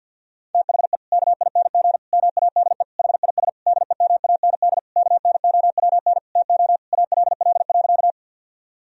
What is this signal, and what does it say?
Morse code, heard as a keyed tone, about 50 wpm, 700 Hz. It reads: THE KING MADE HIS DEMAND KNOWN TO ALL.
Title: Sentences from Top 1000 Words - Review Part 4 - 50wpm